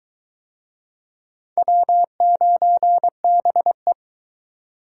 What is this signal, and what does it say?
Morse code, heard as a keyed tone, about 22 wpm, 700 Hz. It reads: W9BE